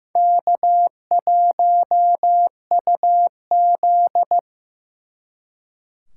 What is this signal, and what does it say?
Morse code, heard as a keyed tone, about 18 wpm, 700 Hz. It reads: K1UZ
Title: Speed Racing - USA 1x2 Call Signs 15wpm